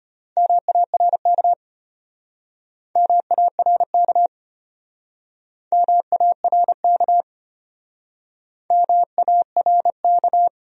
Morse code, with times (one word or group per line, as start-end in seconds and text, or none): 0.4-1.5 MARK
3.0-4.3 MARK
5.7-7.2 MARK
8.7-10.5 MARK